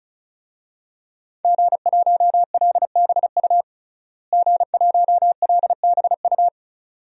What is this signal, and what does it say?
Morse code, heard as a keyed tone, about 36 wpm, 700 Hz. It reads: G1LBU G1LBU
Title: DXCC-100 Call Signs - 5 Characters - Repeated 2x - 35wpm